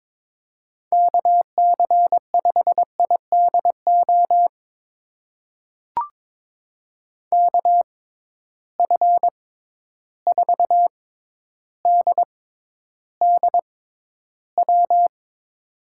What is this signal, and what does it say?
Morse code, heard as a keyed tone, about 22 wpm, spaced as for 11 wpm, 700 Hz. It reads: KC5IDO KF4DDW
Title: USA Call Signs - 6 Characters - 6x Character Spacing - 22wpm